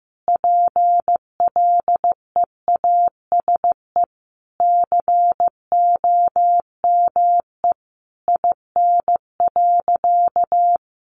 Morse code, heard as a keyed tone, about 15 wpm, 700 Hz. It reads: PLEASE COME IN.
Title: Sentences from Top 1500 Words 15wpm